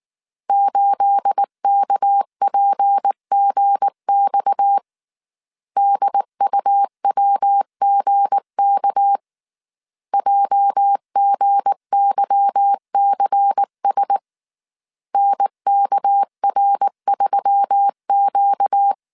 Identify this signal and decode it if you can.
Morse code, heard as a keyed tone, about 19 wpm, 800 Hz. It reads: 8KPGX DUWGK JGYCS NKR3Q
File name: CW / DET / 1 9 1500C S A 18wpm 2459